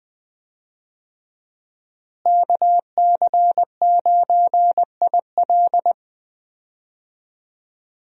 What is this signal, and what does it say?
Morse code, heard as a keyed tone, about 20 wpm, 700 Hz. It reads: KC9IL
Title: USA Call Signs - 5 Characters - Repeated 2x - 20wpm